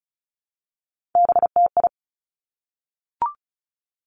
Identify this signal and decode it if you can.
Morse code, heard as a keyed tone, about 36 wpm, 700 Hz. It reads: BTI
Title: Three Challenging Letter-Number Combinations 35wpm